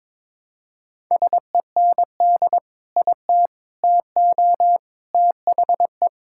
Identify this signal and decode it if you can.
Morse code, heard as a keyed tone, about 22 wpm, 700 Hz. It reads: SEND IT TO THE
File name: practice / 22wpm / Sentences with 3-Word Phrases - 22wpm